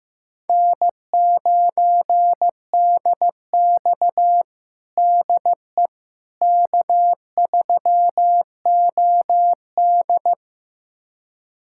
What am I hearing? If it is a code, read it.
Morse code, heard as a keyed tone, about 15 wpm, 700 Hz. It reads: N9DX DE K3OD
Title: USA Callsign DE Callsign - 4 Characters - 15wpm